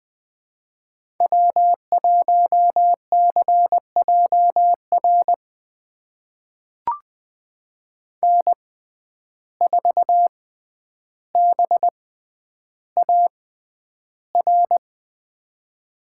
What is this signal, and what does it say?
Morse code, heard as a keyed tone, about 20 wpm, spaced as for 10 wpm, 700 Hz. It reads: W1CJR N4BAR
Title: USA Call Signs - 6x Character Spacing - 20wpm